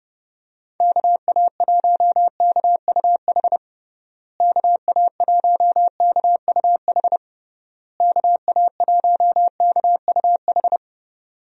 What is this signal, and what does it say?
Morse code, heard as a keyed tone, about 30 wpm, 700 Hz. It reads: KA1KUH KA1KUH KA1KUH
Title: USA Call Signs - 6 Characters - Repeated 3x - 30wpm